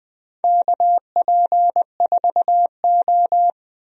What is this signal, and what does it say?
Morse code, heard as a keyed tone, about 20 wpm, 700 Hz. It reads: KP4O
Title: DXCC-100 Call Signs - 4 Characters - 20wpm